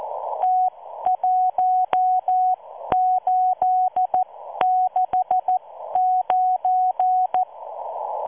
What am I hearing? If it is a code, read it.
Morse code, heard as a keyed tone, about 14 wpm, spaced as for 12 wpm, 740 Hz. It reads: T1869